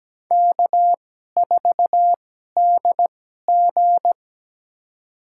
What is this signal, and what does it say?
Morse code, heard as a keyed tone, about 17 wpm, spaced as for 12 wpm, 700 Hz. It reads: K4DG
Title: USA Call Signs - 4 Characters - 2x Character Spacing - 17wpm